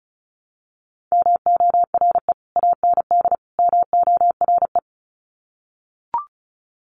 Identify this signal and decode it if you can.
Morse code, heard as a keyed tone, about 34 wpm, 700 Hz. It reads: MORE AND MORE
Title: Sets of 3 Words 35wpm